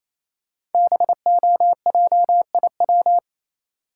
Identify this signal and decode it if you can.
Morse code, heard as a keyed tone, about 27 wpm, 700 Hz. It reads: BOJIW
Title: Code Groups - 5 Letters - 28wpm